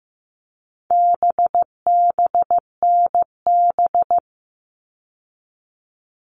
Morse code, heard as a keyed tone, about 15 wpm, 700 Hz. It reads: BBNB
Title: All Cut Numbers - 4-Digit 15wpm